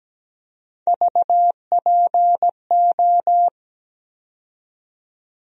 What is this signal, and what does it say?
Morse code, heard as a keyed tone, about 17 wpm, 700 Hz. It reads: VPO
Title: USA Call Sign Postfixes - 3 Letters 17wpm